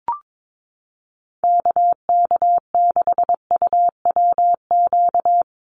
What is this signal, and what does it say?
Morse code, heard as a keyed tone, about 22 wpm, 700 Hz. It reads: KK6UWQ
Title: USA Call Signs 22wpm